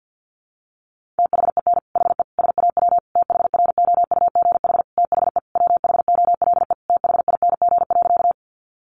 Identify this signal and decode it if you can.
Morse code, heard as a keyed tone, about 48 wpm, 700 Hz. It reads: THEN HE SAW THROUGH THE WHOLE THING.
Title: Sentences from Top 400 Words 50wpm